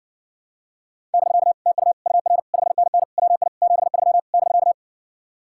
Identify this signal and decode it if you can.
Morse code, heard as a keyed tone, about 60 wpm, 700 Hz. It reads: <BK> TU UR 5NN WI 73 <BK>